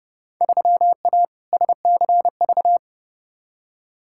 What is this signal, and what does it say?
Morse code, heard as a keyed tone, about 30 wpm, 700 Hz. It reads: 3A SCV
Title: Field Day Exchanges - 30wpm